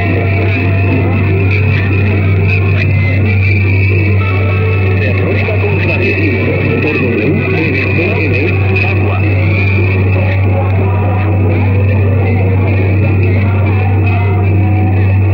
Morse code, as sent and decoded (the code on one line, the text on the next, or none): none